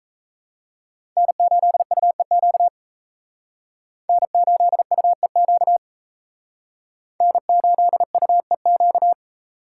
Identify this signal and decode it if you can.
Morse code, heard as a keyed tone, about 35 wpm, 700 Hz. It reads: N8UEQ N8UEQ N8UEQ